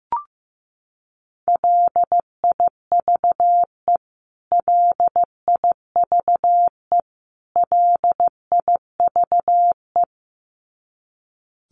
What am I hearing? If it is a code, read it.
Morse code, heard as a keyed tone, about 15 wpm, 700 Hz. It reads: LIVE LIVE LIVE
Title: Top 300 Words Repeated 15wpm